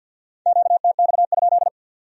Morse code, heard as a keyed tone, about 62 wpm, 700 Hz. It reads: QTX?
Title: Speed-Racing Q-Codes - 50wpm